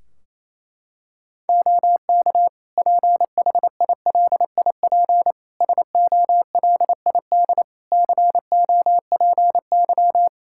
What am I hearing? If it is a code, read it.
Morse code, heard as a keyed tone, about 28 wpm, 700 Hz. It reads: OK PHILIP SOLID COPY
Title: K1USN SST QSO Practice 28wpm